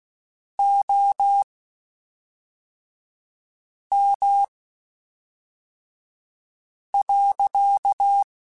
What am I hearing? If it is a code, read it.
Morse code, heard as a keyed tone, about 16 wpm, spaced as for 3 wpm, 800 Hz. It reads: OM.
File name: (spaced) / QSOs Spaced-mp3s / kc7sac @5 sp